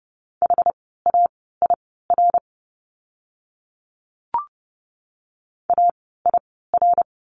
Morse code, heard as a keyed tone, about 30 wpm, spaced as for 17 wpm, 700 Hz. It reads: HAIR AIR
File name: Lesson 17f - Letter u - Words 30wpm